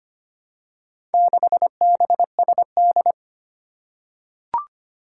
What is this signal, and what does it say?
Morse code, heard as a keyed tone, about 25 wpm, 700 Hz. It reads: 6BSD
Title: Four Challenging Letter-Number Combinations 25wpm